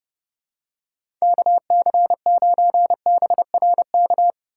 Morse code, heard as a keyed tone, about 30 wpm, 700 Hz. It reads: KC9BRK
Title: USA Call Signs - 4x Character Spacing - 30wpm